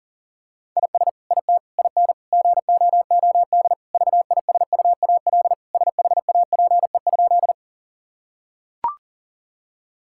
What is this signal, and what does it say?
Morse code, heard as a keyed tone, about 40 wpm, 700 Hz. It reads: IS IT IN GOOD VISUAL SHAPE?
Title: Sentences from Top 1000 Words - Review Part 6 - 40wpm